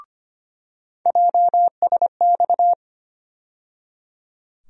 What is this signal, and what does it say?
Morse code, heard as a keyed tone, about 25 wpm, 700 Hz. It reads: JSX
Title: USA Call Sign Postfixes - 3 Letters 25wpm